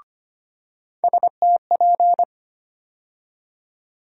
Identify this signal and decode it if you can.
Morse code, heard as a keyed tone, about 25 wpm, 700 Hz. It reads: STP
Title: USA Call Sign Postfixes - 3 Letters 25wpm